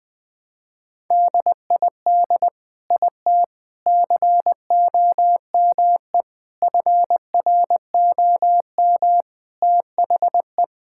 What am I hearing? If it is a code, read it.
Morse code, heard as a keyed tone, about 20 wpm, 700 Hz. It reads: DID IT COME FROM THE
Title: Sentences from Top 1000 Words - Review Part 5 - 20wpm